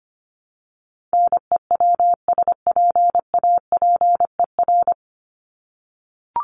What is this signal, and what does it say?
Morse code, heard as a keyed tone, about 25 wpm, 700 Hz. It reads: NEWSPAPER
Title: Top 751-1000 Words 25wpm